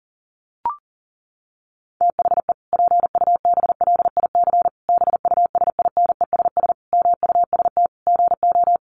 Morse code, heard as a keyed tone, about 40 wpm, 700 Hz. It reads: THE PUBLIC BUSINESS MUST GO
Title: Sentences from Top 1300 Words 40wpm